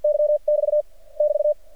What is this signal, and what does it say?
Morse code, heard as a keyed tone, about 53 wpm, 600 Hz. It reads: Y <BT> <BT>
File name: archived / CRSnet